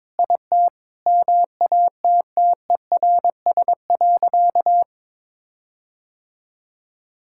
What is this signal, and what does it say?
Morse code, heard as a keyed tone, about 22 wpm, 700 Hz. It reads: IT MATTERS.
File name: practice / 22wpm / Sentences with 4-Word Phrases - 22wpm